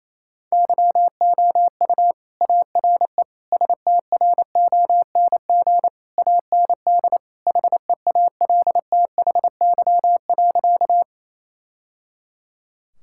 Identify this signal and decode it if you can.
Morse code, heard as a keyed tone, about 27 wpm, 700 Hz. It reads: YOU ARE STRONG AND HEALTHY.
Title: Sentences from Top 1000 Words - Review Part 5 - 28wpm